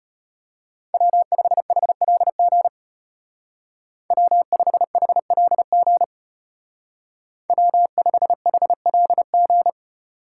Morse code, heard as a keyed tone, about 30 wpm, 700 Hz. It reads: W5HLG W5HLG W5HLG
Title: Speed Racing - USA 1x3 Call Signs 25wpm